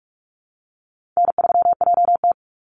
Morse code, heard as a keyed tone, about 44 wpm, 700 Hz. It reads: N3JT
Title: Easy Call Signs Repeated 2x - 45wpm